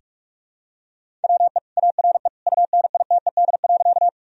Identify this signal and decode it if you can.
Morse code, heard as a keyed tone, about 46 wpm, 700 Hz. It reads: WE ARE UNITED.
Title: Sentences from Top 1000 Words - Review Part 4 - 45wpm